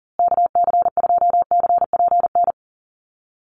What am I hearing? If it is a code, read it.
Morse code, heard as a keyed tone, about 39 wpm, 700 Hz. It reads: KC2CPN
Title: USA Call Signs Repeated 3x 40wpm